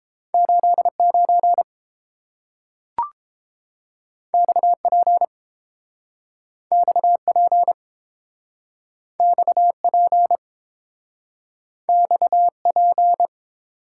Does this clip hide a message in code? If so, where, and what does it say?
Yes: Morse code from 0.3 to 13.3 s, heard as a keyed tone, about 30 wpm, 700 Hz. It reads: 89 XP XP XP XP